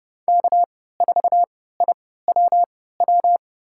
Morse code, heard as a keyed tone, about 30 wpm, spaced as for 17 wpm, 700 Hz. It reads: K4IWW
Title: USA Call Signs - 5 Characters - 3x Character Spacing - 30wpm